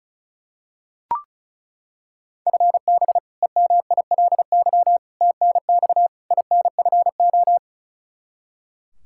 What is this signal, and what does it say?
Morse code, heard as a keyed tone, about 34 wpm, 700 Hz. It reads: FB EMILY TNX INFO
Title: Practice QSO - Exchange 3 - More Info - 35wpm